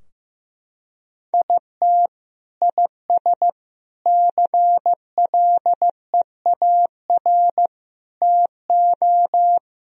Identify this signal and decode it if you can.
Morse code, heard as a keyed tone, about 15 wpm, 700 Hz. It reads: IT IS CLEAR TO
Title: Sentences from Top 1000 Words - Review Part 2 - 15wpm